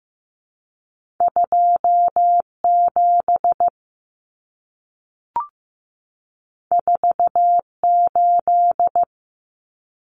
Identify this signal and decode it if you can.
Morse code, heard as a keyed tone, about 15 wpm, 700 Hz. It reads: 27 48
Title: Common Cut Numbers - 2-Digit 15wpm